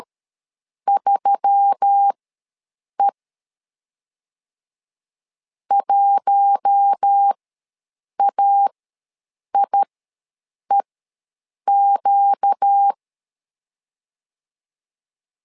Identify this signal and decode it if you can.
Morse code, heard as a keyed tone, about 13 wpm, spaced as for 7 wpm, 800 Hz. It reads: 3E 1AIEQ
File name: LEZIONI / 4 Q1S3 ws 12wpm 2038